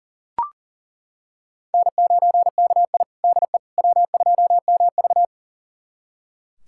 Morse code, heard as a keyed tone, about 40 wpm, 700 Hz. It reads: N9KI DE W2MV
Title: USA Callsign DE Callsign - 4 Characters - 40wpm